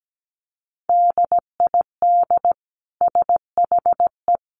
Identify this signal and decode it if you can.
Morse code, heard as a keyed tone, about 17 wpm, 700 Hz. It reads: DID SHE
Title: Sentences from Top 1200 Words 17wpm